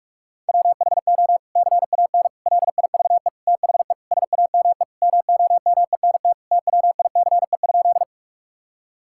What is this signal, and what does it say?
Morse code, heard as a keyed tone, about 41 wpm, 700 Hz. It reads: WHO CAN LIVE THE SAME MOMENT TWICE?
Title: Sentences from Top 1600 Words 45wpm